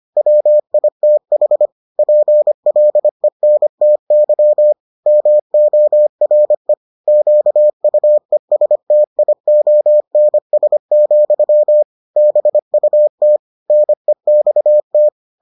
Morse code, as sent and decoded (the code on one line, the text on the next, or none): .-- .. - .... / .--. .-.. -. - -.-- / -- --- .-. / --.- ..- . ... - .. --- -. ... --..-- / -... ..- - / -. -..- -
WITH PLENTY MORE QUESTIONS, BUT NEXT